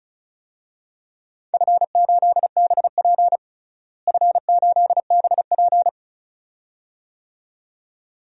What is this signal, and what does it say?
Morse code, heard as a keyed tone, about 34 wpm, 700 Hz. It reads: F8BP F8BP
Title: DXCC-100 Call Signs - 4 Characters - Repeated 2x - 35wpm